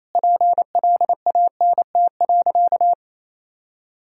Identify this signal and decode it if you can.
Morse code, heard as a keyed tone, about 28 wpm, 700 Hz. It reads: PLANT.
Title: Sentences from Top 1000 Words - Review Part 1 - 28wpm